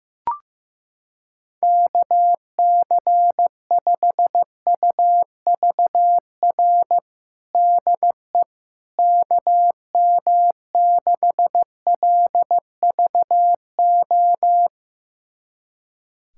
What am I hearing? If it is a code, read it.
Morse code, heard as a keyed tone, about 15 wpm, 700 Hz. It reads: KC5UVR DE KM6LVO
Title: USA Callsign DE Callsign 15wpm